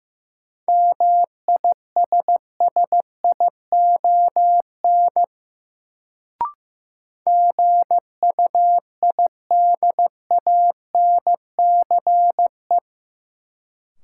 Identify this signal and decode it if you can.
Morse code, heard as a keyed tone, about 15 wpm, 700 Hz. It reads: MISSION GUIDANCE
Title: Sentences from Top 1600 Words - New Word Drill - 15wpm